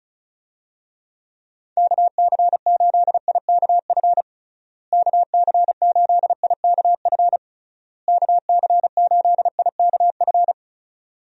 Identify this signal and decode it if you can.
Morse code, heard as a keyed tone, about 36 wpm, 700 Hz. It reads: KC8IKF KC8IKF KC8IKF
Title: USA Call Signs - 6 Characters - Repeated 3x - 35wpm